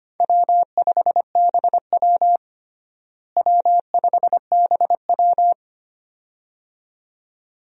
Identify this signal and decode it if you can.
Morse code, heard as a keyed tone, about 25 wpm, 700 Hz. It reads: W5BW W5BW
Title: USA Call Signs - 4 Characters - Repeated 2x - 25wpm